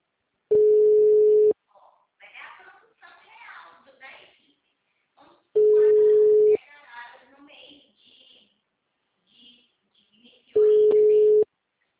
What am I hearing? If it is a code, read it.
Ringback tone (Europe, 425 Hz), with speech.